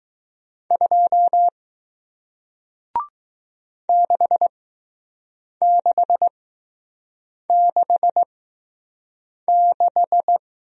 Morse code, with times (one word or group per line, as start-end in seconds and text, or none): 0.7-1.5 2
3.9-10.4 6666